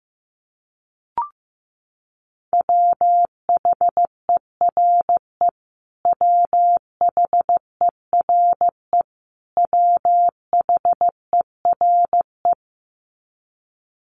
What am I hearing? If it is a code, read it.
Morse code, heard as a keyed tone, about 15 wpm, 700 Hz. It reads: WHERE WHERE WHERE
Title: Top 500 Words Repeated 15wpm